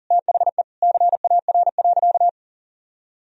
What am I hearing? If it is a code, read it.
Morse code, heard as a keyed tone, about 37 wpm, 700 Hz. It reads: THE CAR.
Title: Sentences from Top 1000 Words - Review Part 1 - 40wpm